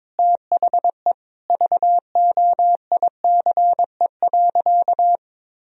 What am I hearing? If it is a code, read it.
Morse code, heard as a keyed tone, about 22 wpm, 700 Hz. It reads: THE VOICE.
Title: Sentences from Top 1000 Words - Review Part 2 - 22wpm